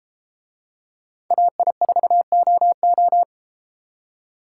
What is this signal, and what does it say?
Morse code, heard as a keyed tone, about 33 wpm, 700 Hz. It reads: AI4OO